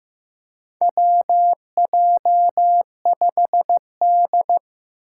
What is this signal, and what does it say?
Morse code, heard as a keyed tone, about 15 wpm, 700 Hz. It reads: WJ5D
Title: USA Call Signs - 4 Characters - 6x Character Spacing - 15wpm